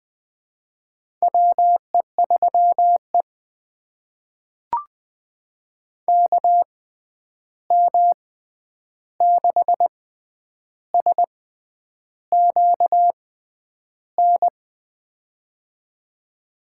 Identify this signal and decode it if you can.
Morse code, heard as a keyed tone, about 20 wpm, spaced as for 9 wpm, 700 Hz. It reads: WE3E KM6SQN
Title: USA Call Signs - 6x Character Spacing - 20wpm